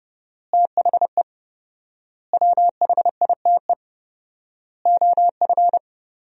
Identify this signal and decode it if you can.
Morse code, heard as a keyed tone, about 30 wpm, 700 Hz. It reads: THE WHITE OF
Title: 4x Word-Spacing - Sentences from Top 500 Words - 30wpm